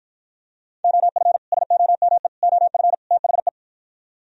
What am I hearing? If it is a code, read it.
Morse code, heard as a keyed tone, about 54 wpm, 700 Hz. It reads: OF SOME OF THE